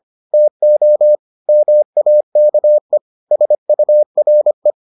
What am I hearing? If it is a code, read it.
Morse code, heard as a keyed tone, about 25 wpm, 600 Hz. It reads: TO MAKE SURE